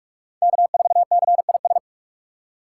Morse code, heard as a keyed tone, about 42 wpm, 700 Hz. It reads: K4CIS
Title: USA Call Signs - 6x Character Spacing - 45wpm